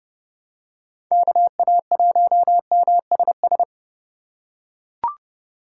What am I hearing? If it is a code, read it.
Morse code, heard as a keyed tone, about 30 wpm, 700 Hz. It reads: KA1MSS
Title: USA Call Signs - 6 Characters - 3x Character Spacing - 30wpm